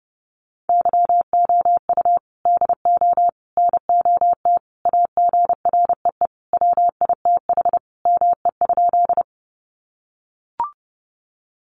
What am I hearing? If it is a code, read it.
Morse code, heard as a keyed tone, about 30 wpm, 700 Hz. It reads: YOU DO NOT AGREE WITH ME?
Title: Sentences from Top 1100 Words 30wpm